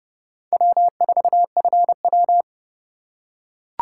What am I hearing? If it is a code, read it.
Morse code, heard as a keyed tone, about 29 wpm, 690 Hz. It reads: W4FW